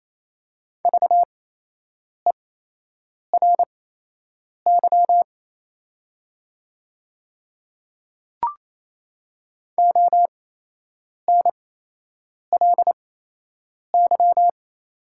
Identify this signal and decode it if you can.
Morse code, heard as a keyed tone, about 28 wpm, spaced as for 8 wpm, 700 Hz. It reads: VERY ONLY
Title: 8x Character Spacing - Top 500 Words - 28wpm